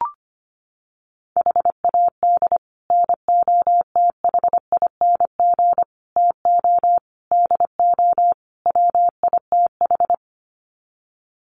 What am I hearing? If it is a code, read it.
Morse code, heard as a keyed tone, about 25 wpm, 700 Hz. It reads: HAD NOTHING TO DO WITH
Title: Sets of 5 Words 25wpm